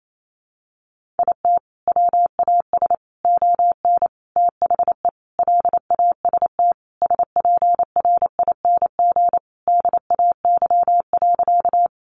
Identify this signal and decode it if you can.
Morse code, heard as a keyed tone, about 28 wpm, 700 Hz. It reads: IT WAS ON THE LAST SPRING DAY.
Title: Sentences from Top 1200 Words 28wpm